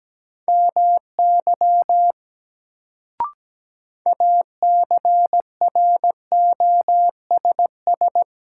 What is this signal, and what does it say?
Morse code, heard as a keyed tone, about 17 wpm, 700 Hz. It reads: MY ACROSS
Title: Sets of 2 Words - New Word Drill - 17wpm